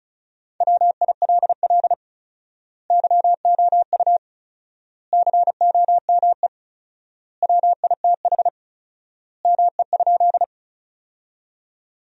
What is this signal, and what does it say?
Morse code, heard as a keyed tone, about 36 wpm, 700 Hz. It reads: WILL YOU COME WITH ME?
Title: 4x Word-Spacing - Sentences from Top 200 Words - 35wpm